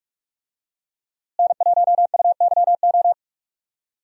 Morse code, heard as a keyed tone, about 44 wpm, 700 Hz. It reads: N1UYO
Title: USA Call Signs - 5 Characters - Repeated 3x - 45wpm